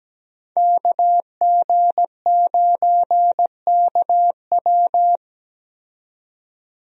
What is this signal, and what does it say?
Morse code, heard as a keyed tone, about 17 wpm, 700 Hz. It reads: KG9KW
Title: USA Call Signs - 5 Characters - 17wpm